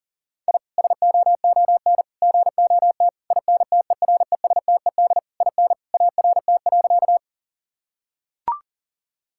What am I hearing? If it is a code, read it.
Morse code, heard as a keyed tone, about 40 wpm, 700 Hz. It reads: I SOON GOT INTERESTED IN ART.
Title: Sentences from Top 1000 Words - Review Part 6 - 40wpm